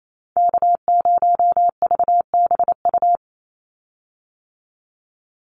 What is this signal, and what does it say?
Morse code, heard as a keyed tone, about 28 wpm, 700 Hz. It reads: K0VBU
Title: Call Signs - 28wpm